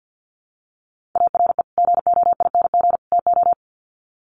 Morse code, heard as a keyed tone, about 50 wpm, 700 Hz. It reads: ARE GOING TO
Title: Sets of 3 Words 50wpm